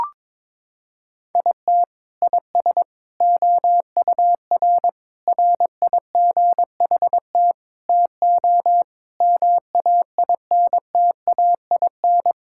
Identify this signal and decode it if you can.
Morse code, heard as a keyed tone, about 22 wpm, 700 Hz. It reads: IT IS OUR RIGHT TO MAINTAIN
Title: Sentences from Top 1000 Words - Review Part 5 - 22wpm